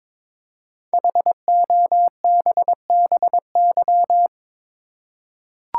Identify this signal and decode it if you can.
Morse code, heard as a keyed tone, about 22 wpm, 700 Hz. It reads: HOBBY
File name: 5 Letter Words - First 3 Letters Given - 22wpm